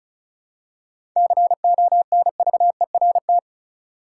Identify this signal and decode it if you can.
Morse code, heard as a keyed tone, about 36 wpm, 700 Hz. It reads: CONVERT
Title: Seven Letter Words - 35wpm